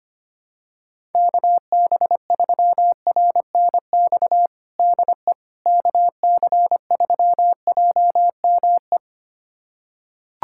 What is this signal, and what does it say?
Morse code, heard as a keyed tone, about 25 wpm, 700 Hz. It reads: KB3RNX DE KC3JME
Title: USA Callsign DE Callsign 25wpm